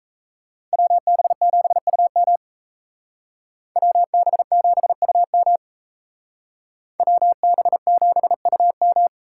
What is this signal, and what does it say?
Morse code, heard as a keyed tone, about 35 wpm, 700 Hz. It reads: WB7UM WB7UM WB7UM